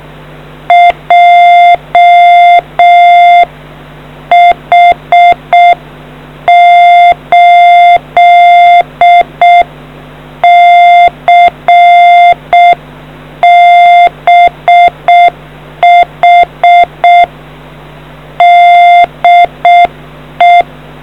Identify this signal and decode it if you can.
Morse code, heard as a keyed tone, about 6 wpm, 700 Hz. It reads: JH8CBH DE